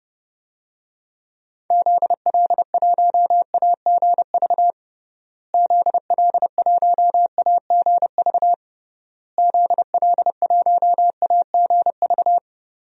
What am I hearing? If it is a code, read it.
Morse code, heard as a keyed tone, about 30 wpm, 700 Hz. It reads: ZL1AGV ZL1AGV ZL1AGV
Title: DXCC-100 Call Signs - 6 Characters - Repeated 3x - 30wpm